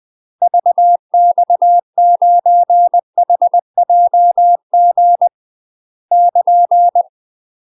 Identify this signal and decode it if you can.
Morse code, heard as a keyed tone, about 20 wpm, 700 Hz. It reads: VX9HJG <KN>
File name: UK MASTER QSOs TRAINING/20WPM